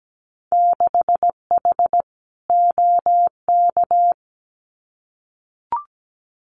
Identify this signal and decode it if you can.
Morse code, heard as a keyed tone, about 17 wpm, 700 Hz. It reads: 6H OK
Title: Winter Field Day Exchanges Encore 17wpm